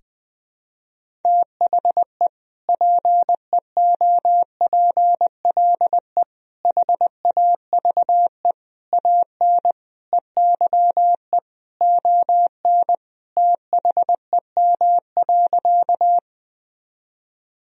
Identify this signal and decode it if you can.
Morse code, heard as a keyed tone, about 20 wpm, 700 Hz. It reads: THE PEOPLE HAVE AN EYE ON THEM.
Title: Sentences from Top 1000 Words - Review Part 1 - 20wpm